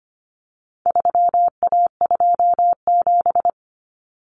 Morse code, heard as a keyed tone, about 25 wpm, 700 Hz. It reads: 3A27
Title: Common Cut Numbers - 4-Digit 25wpm